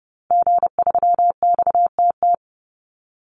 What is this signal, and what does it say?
Morse code, heard as a keyed tone, about 30 wpm, 700 Hz. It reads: G3XTT
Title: Call Signs Repeated 2x - 30wpm